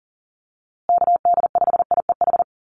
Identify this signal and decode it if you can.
Morse code, heard as a keyed tone, about 40 wpm, 700 Hz. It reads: KD5IEH
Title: USA Call Signs Repeated 2x 40wpm